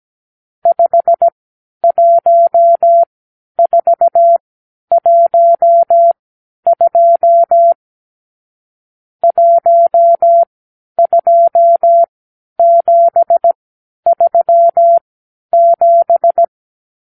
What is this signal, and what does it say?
Morse code, heard as a keyed tone, about 17 wpm, spaced as for 10 wpm, 680 Hz. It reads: 51412 12737
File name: k7qo / Lesson 52 (8)